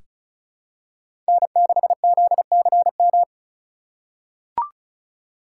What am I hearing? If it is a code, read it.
Morse code, heard as a keyed tone, about 36 wpm, 700 Hz. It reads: N6ZCM
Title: USA Call Signs - 5 Characters - 8x Character Spacing - 35wpm